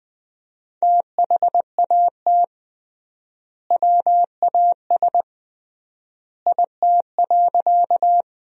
Morse code, heard as a keyed tone, about 20 wpm, 700 Hz. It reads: THAT WAS IT.
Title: 3x Word-Spacing - Sentences from Top 100 Words - 20wpm